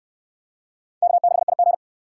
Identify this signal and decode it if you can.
Morse code, heard as a keyed tone, about 69 wpm, 700 Hz. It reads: K6IG